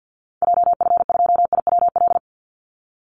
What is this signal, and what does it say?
Morse code, heard as a keyed tone, about 50 wpm, 700 Hz. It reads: JF2IWL
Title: Call Signs Repeated 2x - 50wpm